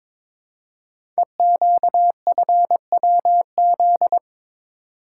Morse code, heard as a keyed tone, about 22 wpm, 700 Hz. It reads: EQFWZ
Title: Code Groups - 5 Letter:Numbers - 22wpm